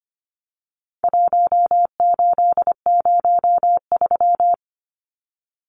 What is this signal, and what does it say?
Morse code, heard as a keyed tone, about 25 wpm, 700 Hz. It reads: 1803
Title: Numbers - 4-Digit 25wpm